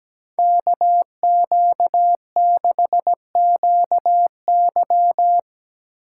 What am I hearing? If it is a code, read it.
Morse code, heard as a keyed tone, about 18 wpm, 700 Hz. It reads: KQ6QY
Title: Speed Racing - USA 2x2 Call Signs 17wpm